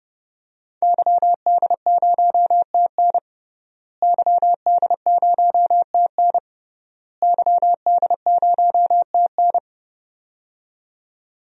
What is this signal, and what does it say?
Morse code, heard as a keyed tone, about 29 wpm, 700 Hz. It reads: YD0TN YD0TN YD0TN
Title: DXCC-100 Call Signs - 5 Characters - Repeated 3x - 30wpm